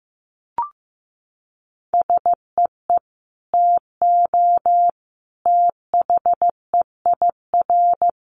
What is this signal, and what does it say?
Morse code, heard as a keyed tone, about 15 wpm, 700 Hz. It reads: SEE TO THEIR
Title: Sentences from Top 1100 Words 15wpm